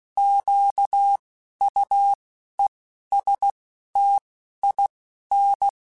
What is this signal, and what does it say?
Morse code, heard as a keyed tone, about 16 wpm, spaced as for 11 wpm, 800 Hz. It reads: QUESTIN
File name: Alphabetical / Alphabetical mp3s / Qs @10